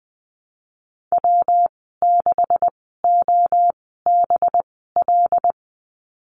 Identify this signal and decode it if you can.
Morse code, heard as a keyed tone, about 20 wpm, spaced as for 14 wpm, 700 Hz. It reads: W6OBL
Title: 186 Lesson 30g - Letter b - Call Signs 20wpm